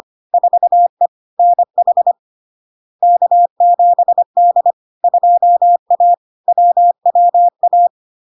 Morse code, heard as a keyed tone, about 25 wpm, 700 Hz. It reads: 4E NH K7D 2A WWA
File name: LARK-FD-2022-Log-in-CW-25wpm-700-Hz